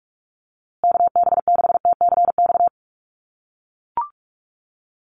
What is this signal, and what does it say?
Morse code, heard as a keyed tone, about 44 wpm, 700 Hz. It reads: KB6TCX
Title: USA Call Signs Repeated 2x 45wpm